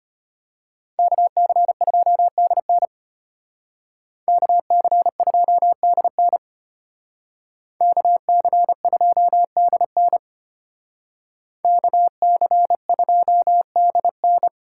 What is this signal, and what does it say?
Morse code, heard as a keyed tone, about 32 wpm, 700 Hz. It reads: KC2DN KC2DN KC2DN KC2DN